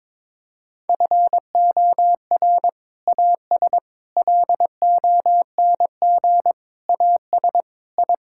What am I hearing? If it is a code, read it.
Morse code, heard as a keyed tone, about 22 wpm, 700 Hz. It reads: FOR AS LONG AS I
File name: practice / 22wpm / Sentences with 8-Word Phrases - 22wpm